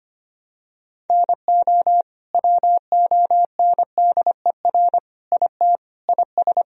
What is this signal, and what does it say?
Morse code, heard as a keyed tone, about 25 wpm, 700 Hz. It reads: NO WONDER IT IS
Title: Sentences from Top 1000 Words - Review Part 3 - 25wpm